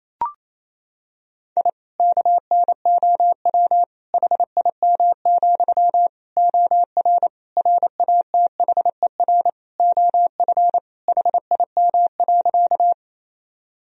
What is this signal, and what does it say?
Morse code, heard as a keyed tone, about 28 wpm, 700 Hz. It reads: I KNOW HIM, OR RATHER OF HIM.
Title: Sentences from Top 1000 Words - Review Part 4 - 28wpm